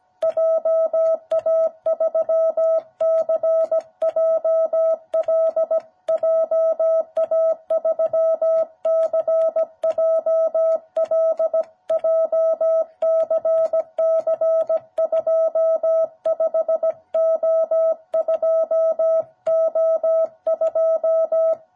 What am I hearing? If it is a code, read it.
Morse code, heard as a keyed tone, about 17 wpm, 660 Hz. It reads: JA3CJL JA3CJL JCC25O2O2
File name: jh8cbh / 2021cw